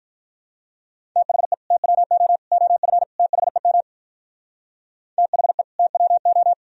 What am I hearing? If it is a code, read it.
Morse code, heard as a keyed tone, about 53 wpm, 700 Hz. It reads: THE TWO OF THEM THE TWO